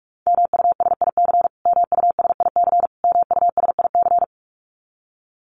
Morse code, heard as a keyed tone, about 44 wpm, 700 Hz. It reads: MUSIC MUSIC MUSIC
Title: Top 500 Words Repeated 45wpm